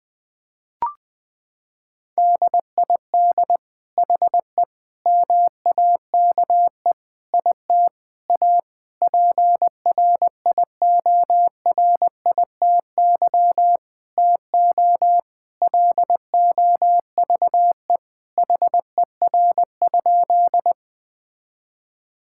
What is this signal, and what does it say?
Morse code, heard as a keyed tone, about 20 wpm, 700 Hz. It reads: DID HE MAKE IT A PRIORITY TO LOVE HER?
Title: Sentences from Top 1600 Words 20wpm